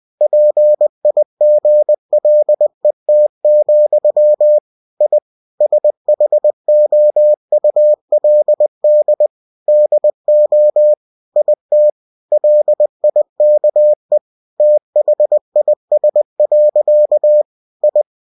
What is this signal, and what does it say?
Morse code, heard as a keyed tone, about 20 wpm, 600 Hz. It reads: PIGLET, I SHOULD DO IT LIKE THIS. I
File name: cw-books / winnie-the-pooh / winnie-20wpm-page-0036